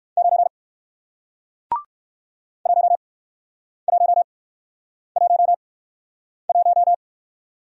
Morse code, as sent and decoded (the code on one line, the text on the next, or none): ----. / .---- .---- .---- .----
9 1111